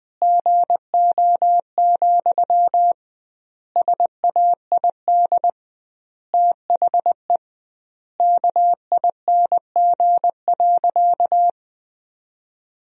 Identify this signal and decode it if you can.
Morse code, heard as a keyed tone, about 20 wpm, 700 Hz. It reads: GO, SAID THE KING.
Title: 2x Word-Spacing - Sentences from Top 400 Words - 20wpm